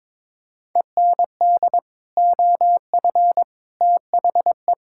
Morse code, heard as a keyed tone, about 21 wpm, 700 Hz. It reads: END OF THE
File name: practice / 22wpm / Sets of 8 Words - 22wpm